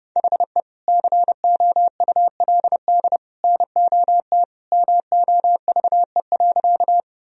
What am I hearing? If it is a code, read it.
Morse code, heard as a keyed tone, about 32 wpm, 700 Hz. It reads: HE COULD NOT MOVE.
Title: Sentences from Top 1000 Words - Review Part 1 - 30wpm